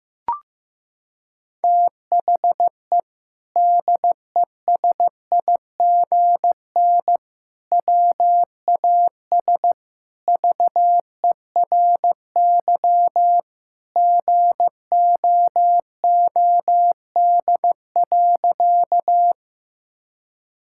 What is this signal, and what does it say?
Morse code, heard as a keyed tone, about 15 wpm, 700 Hz. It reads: THE DESIGN WAS VERY GOOD.
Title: Sentences from Top 1000 Words - Review Part 5 - 15wpm